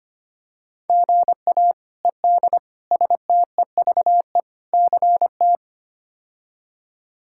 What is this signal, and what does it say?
Morse code, heard as a keyed tone, about 25 wpm, 700 Hz. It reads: GA ED STEVE CT
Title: K1USN SST QSO Practice - 25wpm